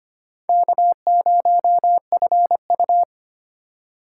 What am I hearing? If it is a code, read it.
Morse code, heard as a keyed tone, about 25 wpm, 700 Hz. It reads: K0FU
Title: USA Call Signs - 4 Characters - 4x Character Spacing - 25wpm